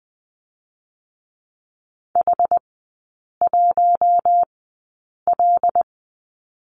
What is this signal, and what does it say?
Morse code, heard as a keyed tone, about 20 wpm, spaced as for 8 wpm, 700 Hz. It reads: H1L
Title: 127 Lesson 21e - Number 6 - Three Characters 20wpm